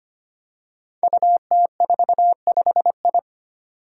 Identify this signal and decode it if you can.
Morse code, heard as a keyed tone, about 25 wpm, 700 Hz. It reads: UT45I
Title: Five Challenging Letter-Number Combinations 25wpm